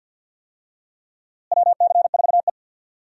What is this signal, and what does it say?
Morse code, heard as a keyed tone, about 52 wpm, 700 Hz. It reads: WC4E
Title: USA Call Signs - 4 Characters - Repeated 3x - 50wpm